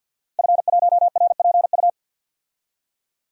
Morse code, heard as a keyed tone, about 52 wpm, 700 Hz. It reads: F1RPU
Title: DXCC-100 Call Signs - 5 Characters - Repeated 2x - 50wpm